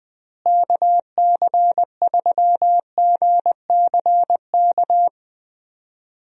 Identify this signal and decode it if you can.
Morse code, heard as a keyed tone, about 20 wpm, 700 Hz. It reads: KC3GCK
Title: USA Call Signs - 6 Characters - Repeated 3x - 20wpm